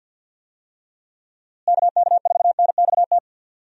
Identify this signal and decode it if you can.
Morse code, heard as a keyed tone, about 47 wpm, 700 Hz. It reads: KK4NXT
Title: USA Call Signs - 6 Characters - Repeated 2x - 50wpm